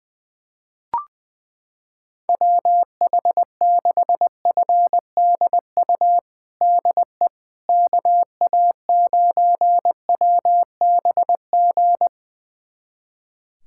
Morse code, heard as a keyed tone, about 20 wpm, 700 Hz. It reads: WH6FDU DE KA9WBG
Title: USA Callsign DE Callsign - 6 Characters - 20wpm